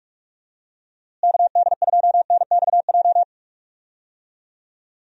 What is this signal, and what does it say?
Morse code, heard as a keyed tone, about 46 wpm, 700 Hz. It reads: KD2NXJ
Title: USA Call Signs - 6 Characters - Repeated 2x - 45wpm